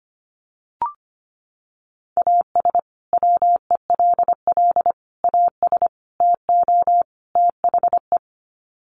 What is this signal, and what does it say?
Morse code, heard as a keyed tone, about 25 wpm, 700 Hz. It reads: AS WELL AS TO THE
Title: Sets of 5 Words Encore2 25wpm